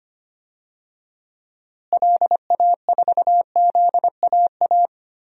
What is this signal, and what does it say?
Morse code, heard as a keyed tone, about 25 wpm, 700 Hz. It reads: LA4ZAA
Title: DXCC-100 Call Signs - 6 Characters - Repeated 2x - 25wpm